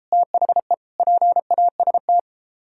Morse code, heard as a keyed tone, about 33 wpm, 700 Hz. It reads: THE PAST